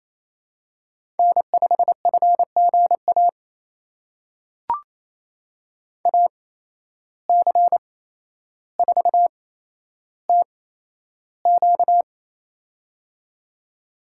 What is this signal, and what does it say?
Morse code, heard as a keyed tone, about 28 wpm, spaced as for 12 wpm, 700 Hz. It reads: N5FGA AC4TQ
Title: USA Call Signs - 5 Characters - 8x Character Spacing - 28wpm